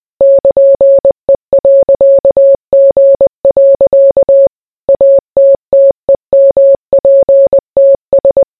Morse code, heard as a keyed tone, about 20 wpm, 550 Hz. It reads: <KN>E.G. ATTEMPTS